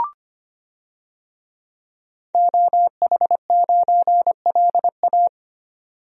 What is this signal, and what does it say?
Morse code, heard as a keyed tone, about 25 wpm, 700 Hz. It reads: OH9LA
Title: DXCC-100 Call Signs - 5 Characters - 25wpm